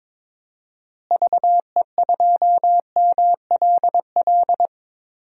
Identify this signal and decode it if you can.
Morse code, heard as a keyed tone, about 22 wpm, 700 Hz. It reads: VE2MLL
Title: DXCC-100 Call Signs - 6 Characters - Repeated 2x - 22wpm